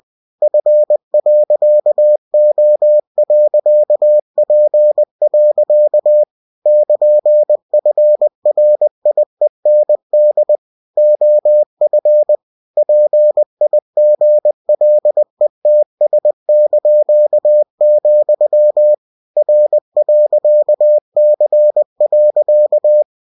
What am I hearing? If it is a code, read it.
Morse code, heard as a keyed tone, about 20 wpm, 600 Hz. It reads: F.O.P. <KN>FRIEND OF PIGLETS), R.C.
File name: cw-books / winnie-the-pooh / winnie-20wpm-page-0099